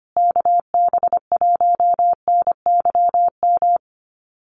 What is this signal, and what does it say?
Morse code, heard as a keyed tone, about 25 wpm, 700 Hz. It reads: KB1NYM
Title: USA Call Signs Repeated 2x 25wpm